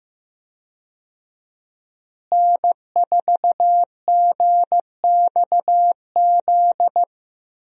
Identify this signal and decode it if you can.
Morse code, heard as a keyed tone, about 15 wpm, 700 Hz. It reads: N4GXZ
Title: USA Call Signs - 5 Characters - Repeated 3x - 15wpm